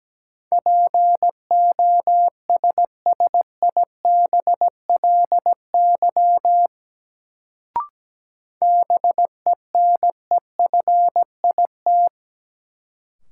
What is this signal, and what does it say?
Morse code, heard as a keyed tone, about 17 wpm, 700 Hz. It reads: POSSIBLY BENEFIT
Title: Sentences from Top 1300 Words - New Word Drill Alt - 17wpm